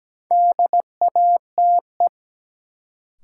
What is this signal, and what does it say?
Morse code, heard as a keyed tone, about 17 wpm, 700 Hz. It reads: DATE
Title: Sentences from Top 1100 Words - New Word Drill Alt - 17wpm